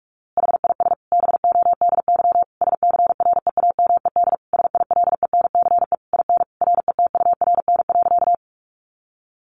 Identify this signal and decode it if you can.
Morse code, heard as a keyed tone, about 44 wpm, 700 Hz. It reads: HIS BODY SCREAMED SILENCE IN RETURN.
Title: A Sound Of Thunder 45wpm